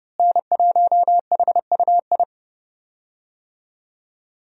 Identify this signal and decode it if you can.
Morse code, heard as a keyed tone, about 30 wpm, 700 Hz. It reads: N1HUI
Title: USA Call Signs - 5 Characters - Repeated 3x - 30wpm